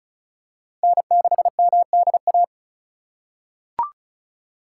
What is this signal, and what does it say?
Morse code, heard as a keyed tone, about 36 wpm, 700 Hz. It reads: N6MDA
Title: USA Call Signs - 5 Characters - 2x Character Spacing - 35wpm